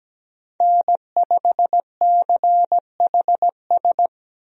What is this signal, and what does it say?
Morse code, heard as a keyed tone, about 17 wpm, 700 Hz. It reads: N5CHS
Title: USA Call Signs - 5 Characters - Repeated 3x - 17wpm